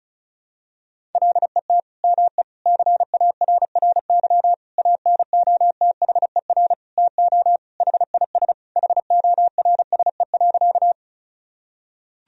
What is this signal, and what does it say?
Morse code, heard as a keyed tone, about 36 wpm, 700 Hz. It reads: LET ME CARRY ANOTHER TO HIS HORSE.
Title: Sentences from Top 1000 Words - Review Part 1 - 35wpm